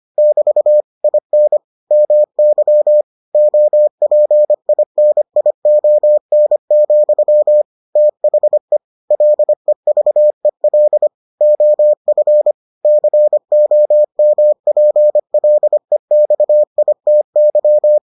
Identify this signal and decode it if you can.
Morse code, heard as a keyed tone, about 25 wpm, 600 Hz. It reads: <BT> IN MY OPINION, THE LEVEL OF COMPLEXITY